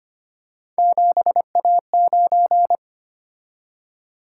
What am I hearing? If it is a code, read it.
Morse code, heard as a keyed tone, about 25 wpm, 700 Hz. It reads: 7A9
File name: Three Letter-Number Combinations - 25wpm